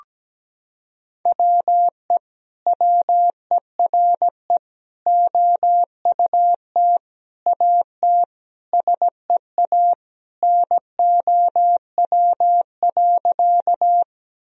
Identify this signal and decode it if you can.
Morse code, heard as a keyed tone, about 17 wpm, 700 Hz. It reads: WE WERE OUT AT SEA NOW.
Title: Sentences from Top 1000 Words - Review Part 1 - 17wpm